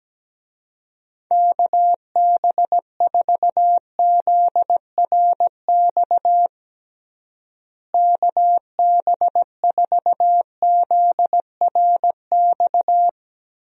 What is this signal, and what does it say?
Morse code, heard as a keyed tone, about 17 wpm, 700 Hz. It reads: KB4ZRX KB4ZRX
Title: USA Call Signs - 6 Characters - Repeated 2x - 17wpm